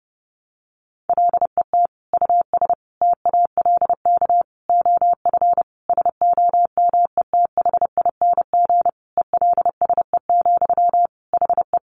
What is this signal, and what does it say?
Morse code, heard as a keyed tone, about 28 wpm, 700 Hz. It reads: LET US TALK OF SOMETHING ELSE, HE
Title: Sentences from Top 800 Words 30wpm